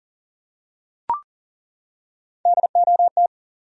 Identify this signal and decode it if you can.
Morse code, heard as a keyed tone, about 40 wpm, 700 Hz. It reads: DOT
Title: USA Common Names - Max 4 Letters - 40wpm